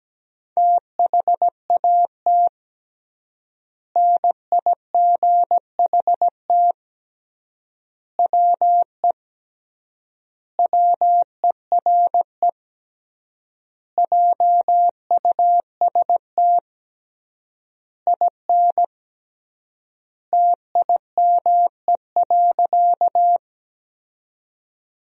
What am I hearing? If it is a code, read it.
Morse code, heard as a keyed tone, about 17 wpm, 700 Hz. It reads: THAT NIGHT WE WERE JUST IN TIME.
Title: 3x Word-Spacing - Sentences from Top 300 Words - 17wpm